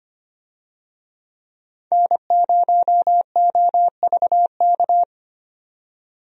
Morse code, heard as a keyed tone, about 25 wpm, 700 Hz. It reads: N0OVK
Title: USA Call Signs - 5 Characters - 25wpm